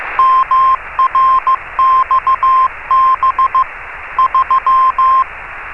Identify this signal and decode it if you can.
Morse code, heard as a keyed tone, about 15 wpm, 1050 Hz. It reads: MRXB 3
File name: RS30-CW-Beacon